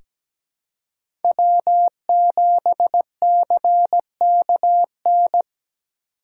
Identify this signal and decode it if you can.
Morse code, heard as a keyed tone, about 17 wpm, 700 Hz. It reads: W7CKN
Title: USA Call Signs - 4x Character Spacing - 17wpm